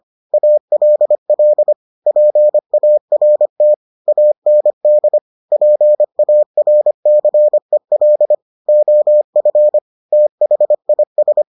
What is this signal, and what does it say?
Morse code, heard as a keyed tone, about 25 wpm, 600 Hz. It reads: ALL PART AND PARCEL OF THIS